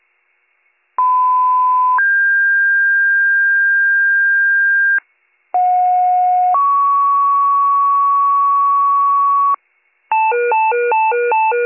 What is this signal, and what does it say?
Two-tone page, heard as a one-second tone, then a three-second tone, about 1010 then 1600 Hz, about 730 then 1120 Hz. It is heard from 1.0 to 9.6 s.